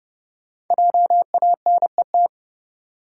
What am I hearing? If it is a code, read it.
Morse code, heard as a keyed tone, about 30 wpm, 700 Hz. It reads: JANET
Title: USA Common Names - Max 5 Letters - 30wpm